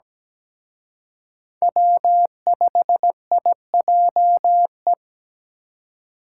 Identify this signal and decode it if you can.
Morse code, heard as a keyed tone, about 20 wpm, 700 Hz. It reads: W5IJE
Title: Speed Racing - USA 1x3 Call Signs 17wpm